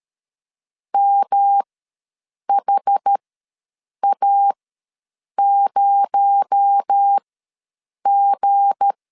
Morse code, heard as a keyed tone, about 13 wpm, spaced as for 7 wpm, 800 Hz. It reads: MHA0G
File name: LEZIONI / 7 GF9Z / 7 GF9 ws 12wpm 3344